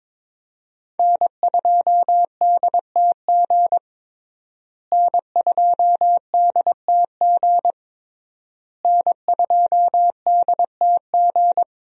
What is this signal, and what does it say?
Morse code, heard as a keyed tone, about 22 wpm, 700 Hz. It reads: N2DTG N2DTG N2DTG
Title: USA Call Signs - 5 Characters - Repeated 3x - 22wpm